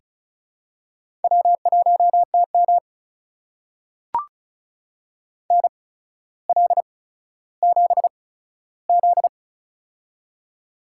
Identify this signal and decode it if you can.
Morse code, heard as a keyed tone, about 36 wpm, spaced as for 15 wpm, 700 Hz. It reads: W1TM NL7Z
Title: USA Call Signs - 4 Characters - 8x Character Spacing - 35wpm